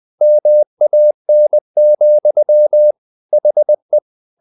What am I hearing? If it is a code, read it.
Morse code, heard as a keyed tone, about 20 wpm, 600 Hz. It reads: MAN, HE
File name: the old man and the sea part 2 20wpm